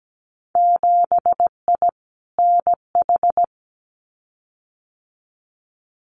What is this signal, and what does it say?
Morse code, heard as a keyed tone, about 17 wpm, 700 Hz. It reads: 7I NH
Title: Winter Field Day Exchanges Encore 17wpm